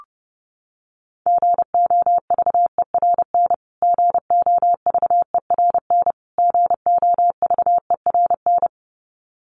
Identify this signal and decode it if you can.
Morse code, heard as a keyed tone, about 28 wpm, 700 Hz. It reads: GOVERN GOVERN GOVERN
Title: Top 500 Words Repeated 30wpm